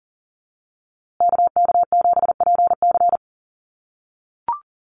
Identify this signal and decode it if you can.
Morse code, heard as a keyed tone, about 40 wpm, 700 Hz. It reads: KK7PC
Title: USA Call Signs Repeated 3x 40wpm